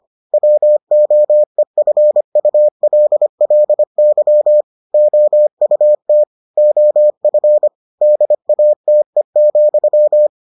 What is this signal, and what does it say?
Morse code, heard as a keyed tone, about 25 wpm, 600 Hz. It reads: WOEFULLY OUT OF DATE,